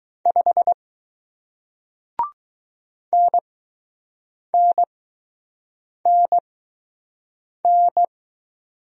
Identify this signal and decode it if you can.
Morse code, heard as a keyed tone, about 20 wpm, spaced as for 6 wpm, 700 Hz. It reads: 5 NNNN